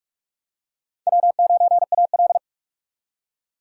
Morse code, heard as a keyed tone, about 41 wpm, 700 Hz. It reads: W9AL
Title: USA Call Signs - 4 Characters - 3x Character Spacing - 45wpm